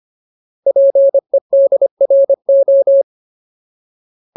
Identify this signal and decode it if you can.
Morse code, heard as a keyed tone, about 25 wpm, 550 Hz. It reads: PEDRO
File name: Ultimate Word List / 25WPM Ultimate